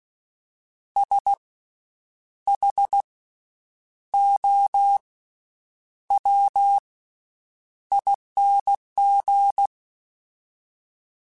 Morse code, heard as a keyed tone, about 16 wpm, spaced as for 8 wpm, 800 Hz. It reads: SHOWING
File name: Special INGs / Special INGs-mp3s / VerySlowToFaster Ing @15